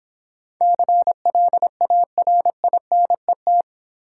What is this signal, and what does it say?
Morse code, heard as a keyed tone, about 25 wpm, 700 Hz. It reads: CLARINET
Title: Easy Words - Speed-Racing with Hints - 20wpm